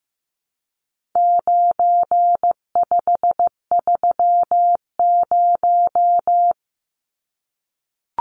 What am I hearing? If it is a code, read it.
Morse code, heard as a keyed tone, about 15 wpm, 700 Hz. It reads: 9530
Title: Numbers - 4-Digit 15wpm